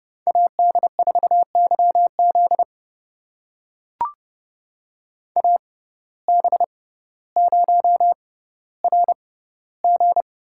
Morse code, heard as a keyed tone, about 30 wpm, spaced as for 15 wpm, 700 Hz. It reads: AD4YZ AB0RG
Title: USA Call Signs - 5 Characters - 6x Character Spacing - 30wpm